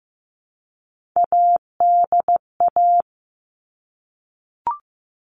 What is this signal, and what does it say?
Morse code, heard as a keyed tone, about 15 wpm, 700 Hz. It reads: ADA
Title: All Cut Numbers - 3-Digit 15wpm